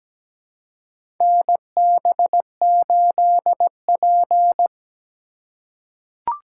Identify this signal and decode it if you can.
Morse code, heard as a keyed tone, about 17 wpm, 700 Hz. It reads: NB8P
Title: USA Call Signs - 4 Characters - 3x Character Spacing - 17wpm